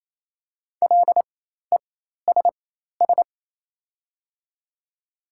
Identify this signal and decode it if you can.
Morse code, heard as a keyed tone, about 28 wpm, spaced as for 13 wpm, 700 Hz. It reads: LESS
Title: 4x Character Spacing - Top 500 Words - 28wpm